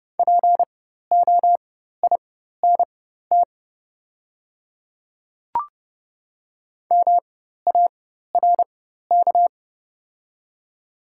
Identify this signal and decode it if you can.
Morse code, heard as a keyed tone, about 30 wpm, spaced as for 14 wpm, 690 Hz. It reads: POINT MARK
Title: 4x Character Spacing - Top 500 Words - 30wpm